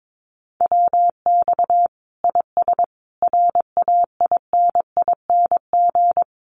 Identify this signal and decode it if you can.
Morse code, heard as a keyed tone, about 22 wpm, 700 Hz. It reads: WX IS RAINING
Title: Common QSO Phrases 22wpm